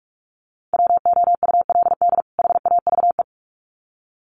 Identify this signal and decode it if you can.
Morse code, heard as a keyed tone, about 44 wpm, 700 Hz. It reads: WOULD HAVE
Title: Sets of 2 Words 45wpm